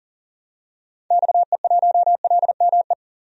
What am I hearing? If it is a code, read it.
Morse code, heard as a keyed tone, about 40 wpm, 700 Hz. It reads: XE1LME
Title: DXCC-100 Call Signs - 6 Characters - 40wpm